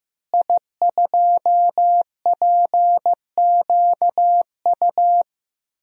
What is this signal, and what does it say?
Morse code, heard as a keyed tone, about 15 wpm, 700 Hz. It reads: I2PQU
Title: DXCC-100 Call Signs - 5 Characters - Repeated 3x - 15wpm